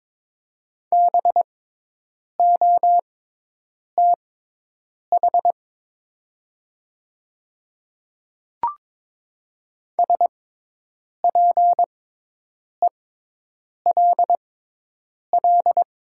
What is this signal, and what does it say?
Morse code, heard as a keyed tone, about 22 wpm, spaced as for 8 wpm, 700 Hz. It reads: BOTH SPELL
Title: 6x Character Spacing - Top 300 Words - 22wpm